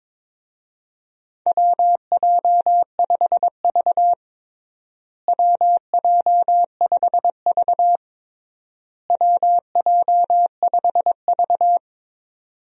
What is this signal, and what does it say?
Morse code, heard as a keyed tone, about 22 wpm, 700 Hz. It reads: WJ5V WJ5V WJ5V
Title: USA Call Signs - 4 Characters - Repeated 3x - 22wpm